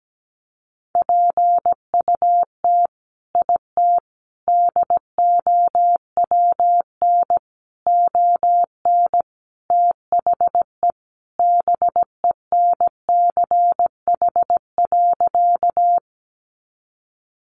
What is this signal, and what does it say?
Morse code, heard as a keyed tone, about 17 wpm, 700 Hz. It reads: PUT IT DOWN ON THE BENCH.
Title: Sentences from Top 1400 Words 17wpm